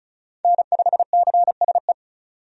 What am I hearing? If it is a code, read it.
Morse code, heard as a keyed tone, about 36 wpm, 700 Hz. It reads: N5CSE
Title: USA Call Signs - 5 Characters - 35wpm